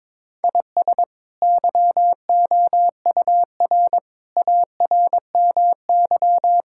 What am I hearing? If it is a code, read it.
Morse code, heard as a keyed tone, about 22 wpm, 700 Hz. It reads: IS YOUR ARMY
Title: Sentences from Top 1000 Words - Review Part 5 - 22wpm